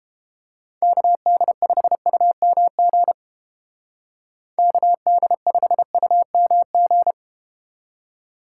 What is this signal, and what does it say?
Morse code, heard as a keyed tone, about 30 wpm, 700 Hz. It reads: KD5UMG KD5UMG